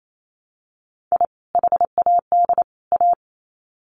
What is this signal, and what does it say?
Morse code, heard as a keyed tone, about 28 wpm, 700 Hz. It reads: I HAD A
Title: Sets of 3 Words 28wpm